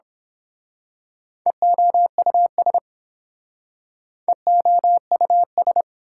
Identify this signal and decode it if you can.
Morse code, heard as a keyed tone, about 27 wpm, 700 Hz. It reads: EOUS EOUS
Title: Speed-Racing Prefixes and Suffixes 22wpm